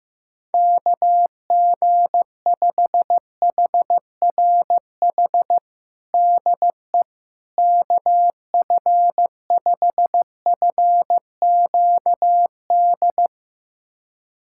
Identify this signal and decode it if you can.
Morse code, heard as a keyed tone, about 15 wpm, 700 Hz. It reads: KG5HRH DE KF5FQD
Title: USA Callsign DE Callsign - 6 Characters - 15wpm